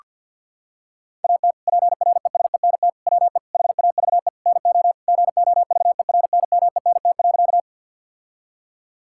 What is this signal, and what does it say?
Morse code, heard as a keyed tone, about 46 wpm, 700 Hz. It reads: AT PRESENT WE HAVE NO GOVERNMENT.
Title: Sentences from Top 1000 Words - Review Part 5 - 50wpm